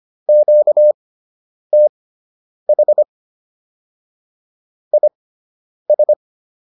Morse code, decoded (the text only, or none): QTH IS